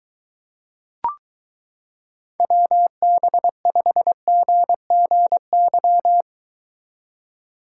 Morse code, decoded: WB5GGY

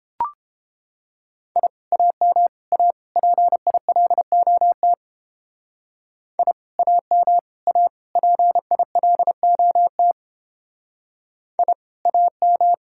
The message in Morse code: I AM A PILOT I AM A PILOT I AM

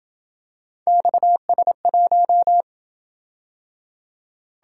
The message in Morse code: XS1

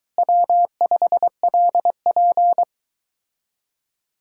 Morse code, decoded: W5LP

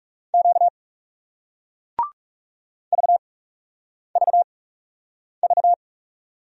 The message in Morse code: Q VVV

